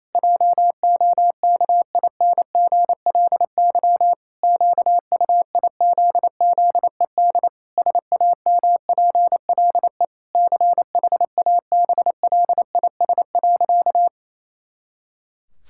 Morse code, heard as a keyed tone, about 28 wpm, 700 Hz. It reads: JOKINGLY QUIZZED SAMPLE CHABLIS.
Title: Pangram Challenge v2 - 28wpm